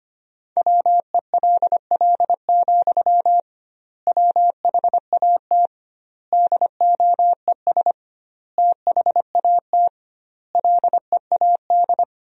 Morse code, heard as a keyed tone, about 25 wpm, 700 Hz. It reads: WELL, WHAT DOES THAT LEAD
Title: 2x Word-Spacing - Sentences from Top 500 Words - 25wpm